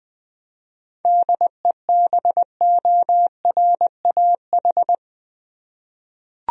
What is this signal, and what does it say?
Morse code, heard as a keyed tone, about 20 wpm, 700 Hz. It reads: DEBORAH
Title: USA Common Names - 20wpm